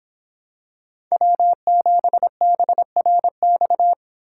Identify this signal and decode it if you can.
Morse code, heard as a keyed tone, about 26 wpm, 700 Hz. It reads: W7BRX